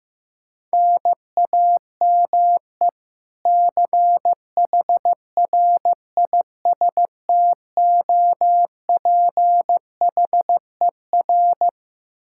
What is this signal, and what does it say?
Morse code, heard as a keyed tone, about 15 wpm, 700 Hz. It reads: NAME CHRISTOPHER
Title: Practice QSO - Exchange 1 - Beginning - 15wpm